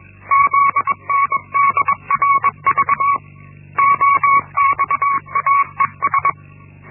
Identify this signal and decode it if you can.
Morse code, heard as a keyed tone, about 21 wpm, 1120 Hz. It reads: ZNDRV OXAES
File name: Linksmakalnis-1